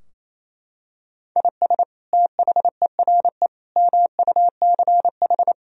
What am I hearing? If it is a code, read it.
Morse code, heard as a keyed tone, about 28 wpm, 700 Hz. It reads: IS THERE MUCH
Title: Sentences from Top 1000 Words - Review Part 4 - 28wpm